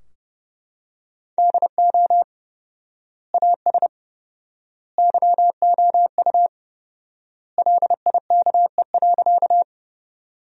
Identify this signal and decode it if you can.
Morse code, heard as a keyed tone, about 32 wpm, 700 Hz. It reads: DO AS YOU LIKE.
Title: 4x Word-Spacing - Sentences from Top 100 Words - 30wpm